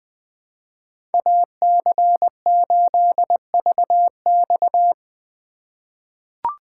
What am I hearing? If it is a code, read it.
Morse code, heard as a keyed tone, about 20 wpm, 700 Hz. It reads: AC8VX
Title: USA Call Signs - 5 Characters - 3x Character Spacing - 20wpm